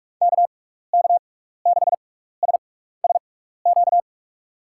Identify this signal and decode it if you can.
Morse code, heard as a keyed tone, about 46 wpm, spaced as for 15 wpm, 700 Hz. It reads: KK6SSQ
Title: USA Call Signs - 6 Characters - 6x Character Spacing - 45wpm